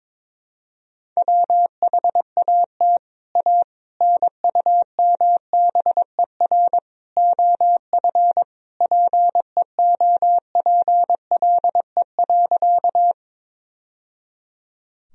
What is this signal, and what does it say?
Morse code, heard as a keyed tone, about 22 wpm, 700 Hz. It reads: WHAT A NUMBER OF PEOPLE.
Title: Sentences from Top 1000 Words - Review Part 1 - 22wpm